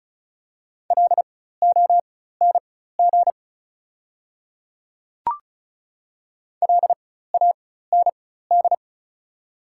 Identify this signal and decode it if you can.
Morse code, heard as a keyed tone, about 36 wpm, spaced as for 16 wpm, 690 Hz. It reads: LONG LAND
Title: 4x Character Spacing - Top 300 Words - 35wpm